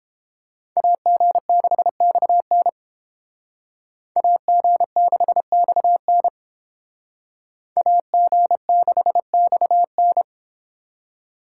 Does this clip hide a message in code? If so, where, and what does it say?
Yes: Morse code from 0.8 to 10.2 s, heard as a keyed tone, about 29 wpm, 700 Hz. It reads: AG6XN AG6XN AG6XN